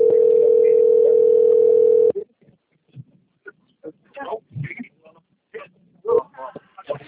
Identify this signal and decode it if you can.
Ringback tone (North America), 440 + 480 Hz, with speech.
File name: out- - - - - .74224